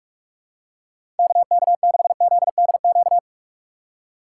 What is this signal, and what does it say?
Morse code, heard as a keyed tone, about 46 wpm, 700 Hz. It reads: KK6ZDQ
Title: USA Call Signs - 6 Characters - 4x Character Spacing - 45wpm